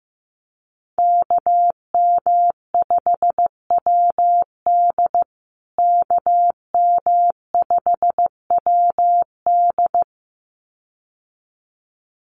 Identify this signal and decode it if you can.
Morse code, heard as a keyed tone, about 15 wpm, 700 Hz. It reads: KM5WD KM5WD
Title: USA Call Signs Repeated 2x 15wpm